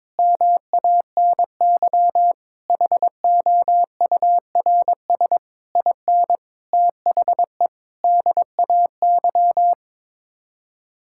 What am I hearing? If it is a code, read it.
Morse code, heard as a keyed tone, about 22 wpm, 700 Hz. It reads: MANY HOURS IN THE DAY
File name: Sets of 9 Words - 22wpm